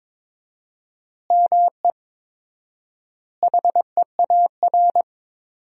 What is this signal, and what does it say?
Morse code, heard as a keyed tone, about 22 wpm, 700 Hz. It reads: ME HEAR
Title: 4x Word-Spacing - Sentences from Top 300 Words - 22wpm